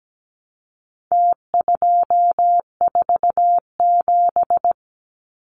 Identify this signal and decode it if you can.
Morse code, heard as a keyed tone, about 17 wpm, 700 Hz. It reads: T247
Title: Numbers - Contest Style - 4-Digit 17wpm